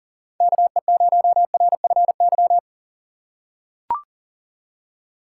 Morse code, heard as a keyed tone, about 40 wpm, 700 Hz. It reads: KE0RFY
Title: USA Call Signs - 6 Characters - 6x Character Spacing - 40wpm